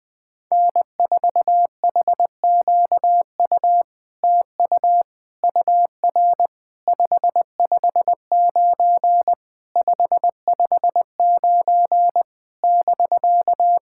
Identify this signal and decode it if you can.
Morse code, heard as a keyed tone, about 20 wpm, 700 Hz. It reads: N4HQU TU UR 559 559 <BK>
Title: SOTA QSO Practice 20wpm